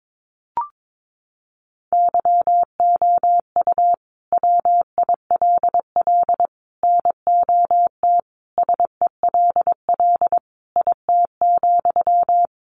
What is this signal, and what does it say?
Morse code, heard as a keyed tone, about 22 wpm, 700 Hz. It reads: YOU WILL NOT SELL IT,
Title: Sentences from Top 1100 Words 22wpm